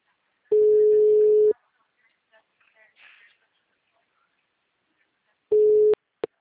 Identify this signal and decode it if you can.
Ringback tone (Europe, 425 Hz), with speech.